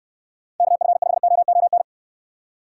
Morse code, heard as a keyed tone, about 69 wpm, 700 Hz. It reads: KF5OON